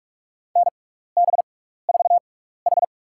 Morse code, heard as a keyed tone, about 46 wpm, spaced as for 15 wpm, 700 Hz. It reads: NB4H
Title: USA Call Signs - 4 Characters - 6x Character Spacing - 45wpm